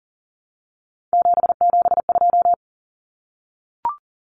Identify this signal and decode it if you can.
Morse code, heard as a keyed tone, about 40 wpm, 700 Hz. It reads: 772
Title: Numbers - 3-Digit 40wpm